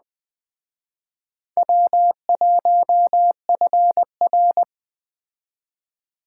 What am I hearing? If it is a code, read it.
Morse code, heard as a keyed tone, about 21 wpm, 700 Hz. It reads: W1FR
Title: Speed Racing - USA 1x2 Call Signs 17wpm